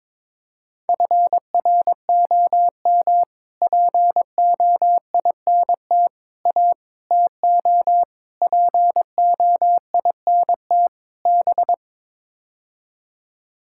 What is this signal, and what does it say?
Morse code, heard as a keyed tone, about 22 wpm, 700 Hz. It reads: FROM POINT A TO POINT B
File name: practice / 22wpm / Sets of 6 Words - 22wpm